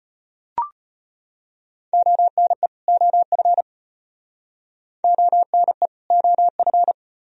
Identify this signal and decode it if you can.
Morse code, heard as a keyed tone, about 33 wpm, 700 Hz. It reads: ONE OF ONE OF